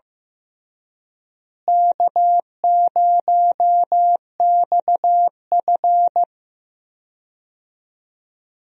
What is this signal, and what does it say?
Morse code, heard as a keyed tone, about 15 wpm, 700 Hz. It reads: K0XF
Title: USA Call Signs - 4 Characters - Repeated 2x - 15wpm